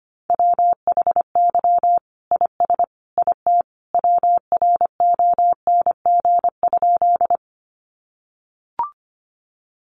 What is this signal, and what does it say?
Morse code, heard as a keyed tone, about 25 wpm, 700 Hz. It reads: WHY IS IT WRONG?
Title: Sentences from Top 900 Words 25wpm